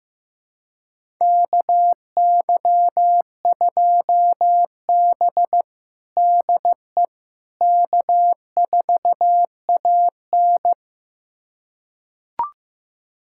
Morse code, heard as a keyed tone, about 15 wpm, 700 Hz. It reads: KY2B DE K4AN
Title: USA Callsign DE Callsign - 4 Characters - 15wpm